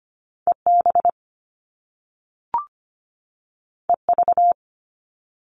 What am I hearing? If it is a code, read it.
Morse code, heard as a keyed tone, about 25 wpm, 700 Hz. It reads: E B E V
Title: Top Two Letter Combinations in Words 25wpm